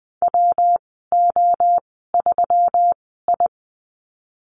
Morse code, heard as a keyed tone, about 20 wpm, spaced as for 15 wpm, 700 Hz. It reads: WO3I
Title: 128 Lesson 21g - Number 6 - Call Signs 20wpm